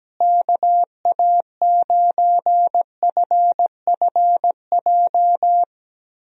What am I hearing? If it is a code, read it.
Morse code, heard as a keyed tone, about 17 wpm, 700 Hz. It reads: KA9FFJ
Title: USA Call Signs - 6 Characters - Repeated 3x - 17wpm